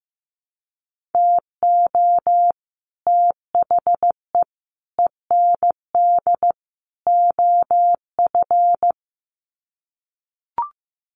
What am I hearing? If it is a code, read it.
Morse code, heard as a keyed tone, about 15 wpm, 700 Hz. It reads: TO THE END OF
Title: Sets of 4 Words 15wpm